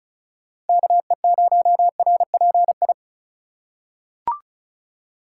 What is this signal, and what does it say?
Morse code, heard as a keyed tone, about 36 wpm, 700 Hz. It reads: KE0RPI
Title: USA Call Signs - 6 Characters - 2x Character Spacing - 35wpm